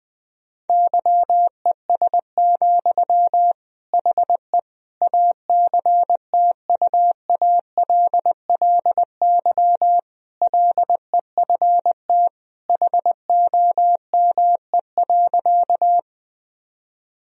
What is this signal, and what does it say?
Morse code, heard as a keyed tone, about 20 wpm, 700 Hz. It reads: YES, HE ACTUALLY LEFT HOME.
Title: Sentences from Top 1000 Words - Review Part 4 - 20wpm